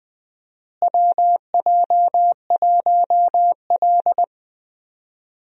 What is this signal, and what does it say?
Morse code, heard as a keyed tone, about 20 wpm, 700 Hz. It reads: WJ1L
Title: USA Call Signs - 4 Characters - Repeated 2x - 20wpm